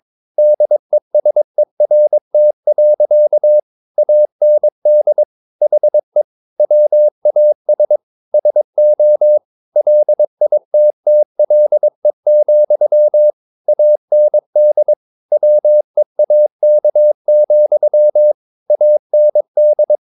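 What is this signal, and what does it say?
Morse code, heard as a keyed tone, about 22 wpm, 600 Hz. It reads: DESERT. AND HE WAS SO LITTLE, AND WEAK, AND
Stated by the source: Morse code practice at 22 wpm, sending text from an English novel.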